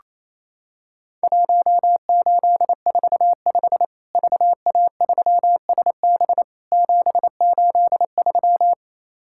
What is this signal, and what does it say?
Morse code, heard as a keyed tone, about 28 wpm, 700 Hz. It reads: 1845 VA3SB 783
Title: Worked All Europe DX Contest - QTC Exchange - 28wpm